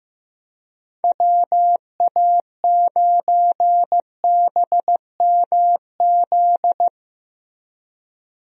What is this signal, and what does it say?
Morse code, heard as a keyed tone, about 15 wpm, 700 Hz. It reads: WA9BMZ
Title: USA Call Signs - 6 Characters - Repeated 2x - 15wpm